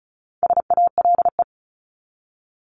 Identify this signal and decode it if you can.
Morse code, heard as a keyed tone, about 34 wpm, 700 Hz. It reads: SALE
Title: Top 1251-1500 Words 35wpm